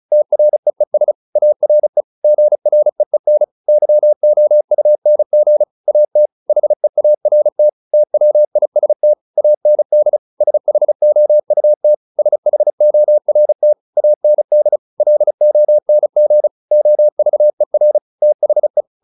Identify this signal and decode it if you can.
Morse code, heard as a keyed tone, about 34 wpm, 600 Hz. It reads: TREES ARE GREEN YOUNG AT HEART TWIST AND SHOUT SHORT AND LONG OVER THE
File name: PR401 35